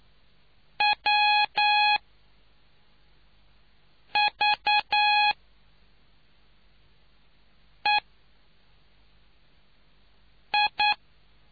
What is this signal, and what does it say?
Morse code, heard as a keyed tone, about 9 wpm, 840 Hz. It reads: W V E I